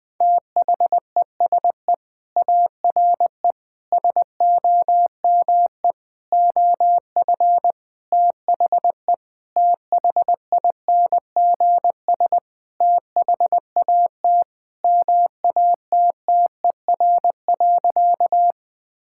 Morse code, heard as a keyed tone, about 20 wpm, 700 Hz. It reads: THESE ARE SOME OF THE THINGS THAT MATTER.
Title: Sentences with 5-Word Phrases - Spoken Beginning - 20wpm